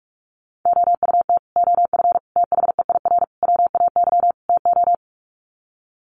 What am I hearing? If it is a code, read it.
Morse code, heard as a keyed tone, about 44 wpm, 700 Hz. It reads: OUT OF THEIR WAY TO
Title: Sets of 5 Words Encore2 45wpm